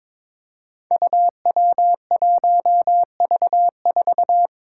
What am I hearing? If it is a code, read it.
Morse code, heard as a keyed tone, about 21 wpm, 700 Hz. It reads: UW1V4
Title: Five Challenging Letter-Number Combinations 22wpm